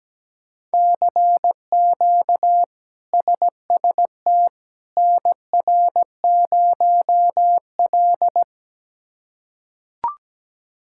Morse code, heard as a keyed tone, about 17 wpm, 700 Hz. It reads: CQ SST NR0L
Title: K1USN SST QSO Practice - 17wpm